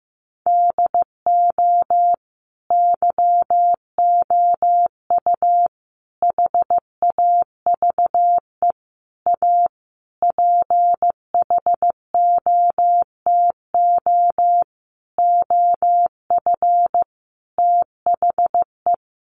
Morse code, heard as a keyed tone, about 15 wpm, 700 Hz. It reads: DO YOU HAVE A PHOTO OF THE
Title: Sentences from Top 1200 Words 15wpm